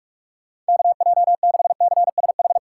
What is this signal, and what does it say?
Morse code, heard as a keyed tone, about 57 wpm, 700 Hz. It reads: KJ6CSH